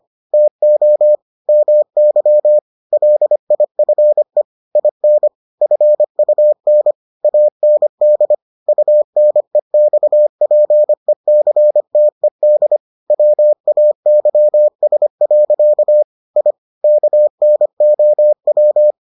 Morse code, heard as a keyed tone, about 25 wpm, 600 Hz. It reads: TO MY LIFE IN FUN AND UNEXPECTED WAYS. I KNOW